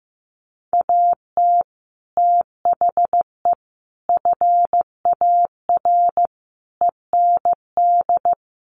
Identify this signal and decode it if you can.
Morse code, heard as a keyed tone, about 15 wpm, 700 Hz. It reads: AT THE FAR END
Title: Sets of 5 Words 15wpm